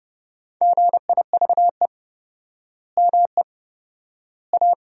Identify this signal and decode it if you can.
Morse code, heard as a keyed tone, about 31 wpm, 700 Hz. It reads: GIVE ME A
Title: 4x Word-Spacing - Sentences from Top 200 Words - 30wpm